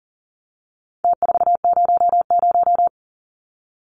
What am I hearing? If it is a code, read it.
Morse code, heard as a keyed tone, about 40 wpm, 700 Hz. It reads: T400
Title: Numbers - Contest Style - 4-Digit 40wpm